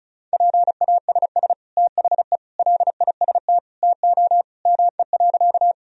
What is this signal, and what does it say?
Morse code, heard as a keyed tone, about 36 wpm, 700 Hz. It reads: PASS THE LIST TO ME.
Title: Sentences from Top 1000 Words - Review Part 2 - 35wpm